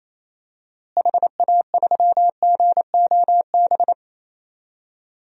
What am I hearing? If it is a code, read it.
Morse code, heard as a keyed tone, about 28 wpm, 700 Hz. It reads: HA3GOB
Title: DXCC-100 Call Signs - 6 Characters - Repeated 3x - 28wpm